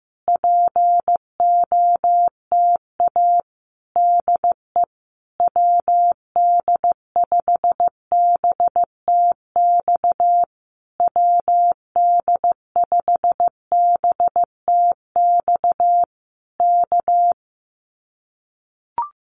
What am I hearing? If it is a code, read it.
Morse code, heard as a keyed tone, about 15 wpm, 700 Hz. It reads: POTA DE WD5BTX WD5BTX K
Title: Common QSO Phrases Encore 15wpm